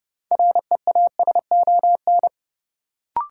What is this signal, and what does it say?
Morse code, heard as a keyed tone, about 29 wpm, 700 Hz. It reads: REASON
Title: Sentences from Top 700 Words - New Word Drill - 30wpm